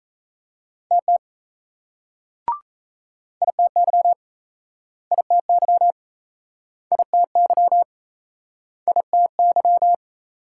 Morse code, heard as a keyed tone, about 38 wpm, 700 Hz. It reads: TT ITY ITY ITY ITY